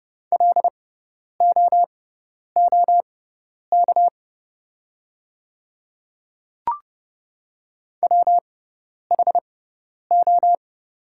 Morse code, heard as a keyed tone, about 30 wpm, spaced as for 10 wpm, 700 Hz. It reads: LOOK WHO
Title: 6x Character Spacing - Top 100 Words - 30wpm